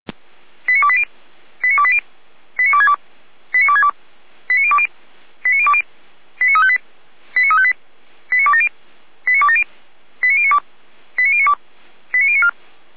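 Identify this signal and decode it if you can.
Five-tone selective call (ZVEI), heard as quick runs of five tones, tones of about 70 ms. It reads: 89280 89280 89262 89262 89020 89020 89358 89358 89270 89270 89092 89092 89094
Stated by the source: ZVEI five-tone selective call received on SDR.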